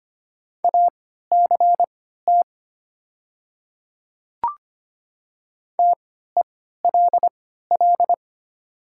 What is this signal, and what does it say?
Morse code, heard as a keyed tone, about 25 wpm, spaced as for 14 wpm, 700 Hz. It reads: ACT TELL
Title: 3x Character Spacing - Top 200 Words - 25wpm